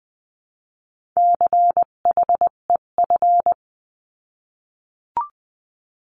Morse code, heard as a keyed tone, about 20 wpm, 700 Hz. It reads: CHEF
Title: 258 Lesson 42c - Words 16-20wpm